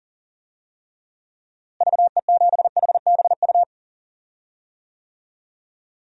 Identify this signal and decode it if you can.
Morse code, heard as a keyed tone, about 37 wpm, 700 Hz. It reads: VE7HBU
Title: DXCC-100 Call Signs - 6 Characters - Repeated 2x - 40wpm